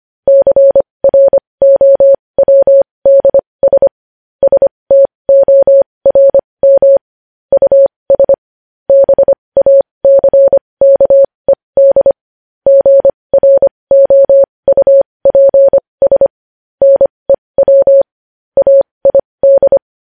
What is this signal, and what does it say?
Morse code, heard as a keyed tone, about 25 wpm, spaced as for 20 wpm, 550 Hz. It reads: CROWDS STORM US BACKED GROUPS NEW AID
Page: News Headlines in Morse Code at 20 WPM